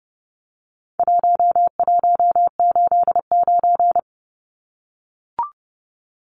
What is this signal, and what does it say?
Morse code, heard as a keyed tone, about 30 wpm, 700 Hz. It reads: 1189
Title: Numbers - 4-Digit 30wpm